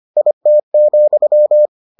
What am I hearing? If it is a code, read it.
Morse code, heard as a keyed tone, about 25 wpm, 600 Hz. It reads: IT,